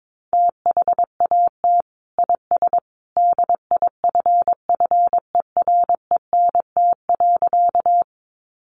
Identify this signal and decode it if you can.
Morse code, heard as a keyed tone, about 22 wpm, 700 Hz. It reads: THAT IS DIFFERENT.
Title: Sentences from Top 600 Words 22wpm